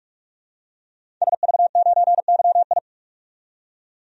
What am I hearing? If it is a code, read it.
Morse code, heard as a keyed tone, about 46 wpm, 700 Hz. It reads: SV9YI
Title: DXCC-100 Call Signs - 5 Characters - 45wpm